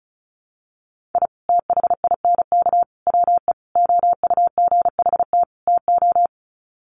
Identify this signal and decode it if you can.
Morse code, heard as a keyed tone, about 34 wpm, 700 Hz. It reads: I THINK WE OUGHT TO
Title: Sets of 5 Words Encore 35wpm